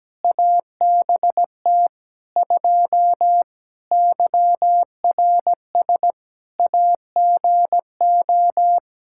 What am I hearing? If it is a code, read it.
Morse code, heard as a keyed tone, about 17 wpm, 700 Hz. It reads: ABT 2 YRS AGO
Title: K1USN SST QSO Practice 17wpm